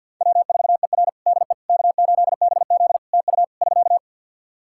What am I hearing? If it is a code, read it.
Morse code, heard as a keyed tone, about 51 wpm, 700 Hz. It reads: W4ER DE K8BZ TU <SK>
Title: Practice QSO - Exchange 4 - Ending - 50wpm